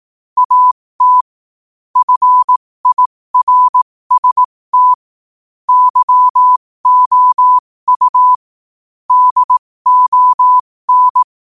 Morse code, heard as a keyed tone, about 18 wpm, spaced as for 15 wpm, 1000 Hz. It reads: AT FIRST YOU DON
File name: qotdfiles / 15wpm